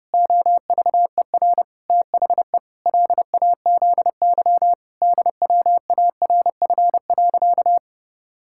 Morse code, heard as a keyed tone, about 30 wpm, 700 Hz. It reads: OVER THE LAZY DWARF.
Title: Pangram Challenge v2 - 30wpm